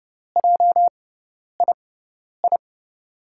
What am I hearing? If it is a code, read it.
Morse code, heard as a keyed tone, about 32 wpm, spaced as for 10 wpm, 700 Hz. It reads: JII
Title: USA Call Signs - 6 Characters - 6x Character Spacing - 30wpm